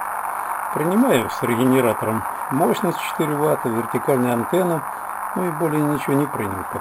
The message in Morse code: none